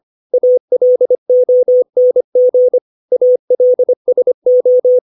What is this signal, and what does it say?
Morse code, heard as a keyed tone, about 25 wpm, 500 Hz. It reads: ALONG ALSO
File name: 632 words / 632 Words 25WPM